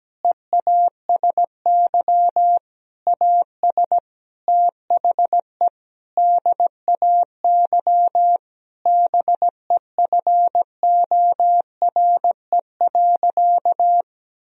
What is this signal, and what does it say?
Morse code, heard as a keyed tone, about 17 wpm, 700 Hz. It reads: EASY AS THE DAY BEFORE.
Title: Sentences from Top 1000 Words - Review Part 1 - 17wpm